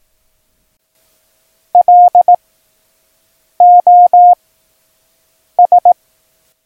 Morse code, heard as a keyed tone, about 18 wpm, spaced as for 6 wpm, 710 Hz. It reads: LOS
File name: acs / Instrument Rating / Instrument ACS